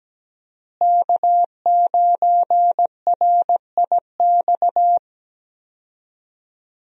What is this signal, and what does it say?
Morse code, heard as a keyed tone, about 17 wpm, 700 Hz. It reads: K9RIX
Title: USA Call Signs - 5 Characters - Repeated 2x - 17wpm